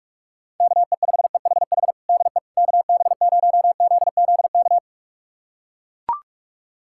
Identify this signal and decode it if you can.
Morse code, heard as a keyed tone, about 41 wpm, 700 Hz. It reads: KE5EHH DE KB0ZZK
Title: USA Callsign DE Callsign - 6 Characters - 45wpm